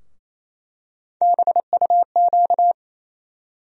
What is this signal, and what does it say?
Morse code, heard as a keyed tone, about 28 wpm, 700 Hz. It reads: BUQ
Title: USA Call Sign Postfixes - 3 Letters 28wpm